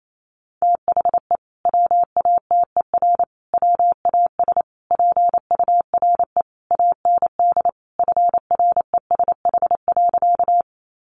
Morse code, heard as a keyed tone, about 28 wpm, 700 Hz. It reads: THE WATER WAS PURE AND FRESH.
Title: Sentences from Top 1500 Words 28wpm